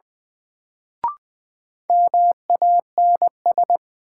Morse code, heard as a keyed tone, about 20 wpm, 700 Hz. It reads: MANS
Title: Sentences from Top 1000 Words - Review Part 6 - New Word Drill - 20wpm